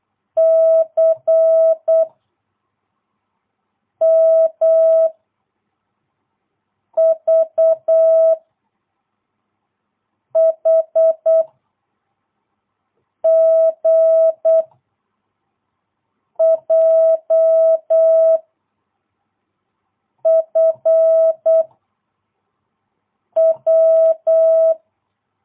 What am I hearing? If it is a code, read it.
Morse code, heard as a keyed tone, about 8 wpm, 660 Hz. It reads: C M V H G J F W
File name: jh8cbh / 2022cw